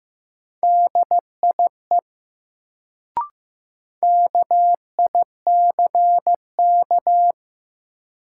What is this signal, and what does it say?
Morse code, heard as a keyed tone, about 15 wpm, 700 Hz. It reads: DIE KICK
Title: Sentences from Top 1300 Words - New Word Drill Alt - 15wpm